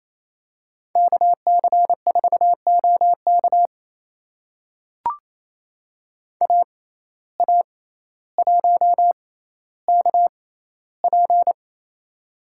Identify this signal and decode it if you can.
Morse code, heard as a keyed tone, about 28 wpm, spaced as for 14 wpm, 700 Hz. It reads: KC4OK AA1KP
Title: USA Call Signs - 5 Characters - 6x Character Spacing - 28wpm